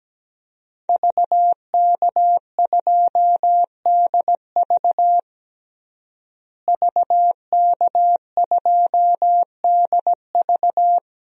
Morse code, heard as a keyed tone, about 17 wpm, 700 Hz. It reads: VK2DV VK2DV
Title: DXCC-100 Call Signs - 5 Characters - Repeated 2x - 17wpm